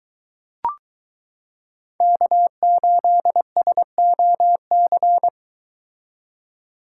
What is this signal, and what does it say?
Morse code, heard as a keyed tone, about 22 wpm, 700 Hz. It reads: K8SOC